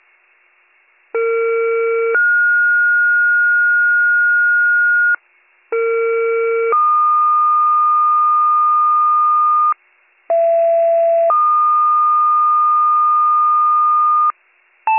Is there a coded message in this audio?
Yes: two-tone page from 1.1 to 14.3 s, heard as a one-second tone, then a three-second tone, about 460 then 1430 Hz, about 460 then 1220 Hz, about 670 then 1220 Hz.